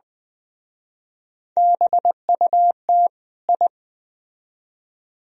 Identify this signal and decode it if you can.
Morse code, heard as a keyed tone, about 20 wpm, 700 Hz. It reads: BUT I
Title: Speed-Racing - Sets of 2 Words 17wpm